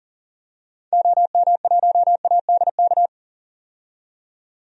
Morse code, heard as a keyed tone, about 40 wpm, 700 Hz. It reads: OM1ADK
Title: DXCC-100 Call Signs - 6 Characters - 40wpm